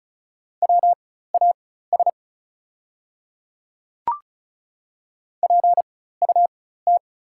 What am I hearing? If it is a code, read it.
Morse code, heard as a keyed tone, about 36 wpm, spaced as for 16 wpm, 700 Hz. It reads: WAS PUT
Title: 4x Character Spacing - Top 200 Words - 35wpm